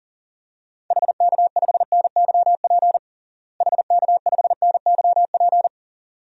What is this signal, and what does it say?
Morse code, heard as a keyed tone, about 40 wpm, 700 Hz. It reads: HK5NYP HK5NYP
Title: DXCC-100 Call Signs - 6 Characters - Repeated 2x - 40wpm